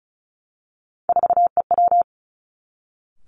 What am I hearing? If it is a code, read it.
Morse code, heard as a keyed tone, about 34 wpm, 700 Hz. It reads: 4EW
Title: Three Challenging Letter-Number Combinations 35wpm